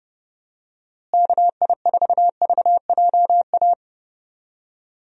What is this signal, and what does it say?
Morse code, heard as a keyed tone, about 30 wpm, 700 Hz. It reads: KI4VJA
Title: USA Call Signs - 6 Characters - Repeated 2x - 30wpm